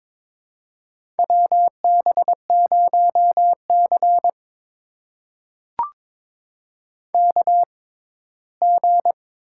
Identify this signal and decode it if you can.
Morse code, heard as a keyed tone, about 22 wpm, spaced as for 15 wpm, 700 Hz. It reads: WB0C KG
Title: USA Call Signs - 6x Character Spacing - 22wpm